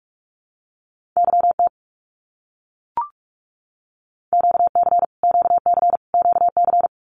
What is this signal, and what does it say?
Morse code, heard as a keyed tone, about 44 wpm, 700 Hz. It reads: YT QC QC QC
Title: Canadian Provinces and Territories - Repeated 3x - 45wpm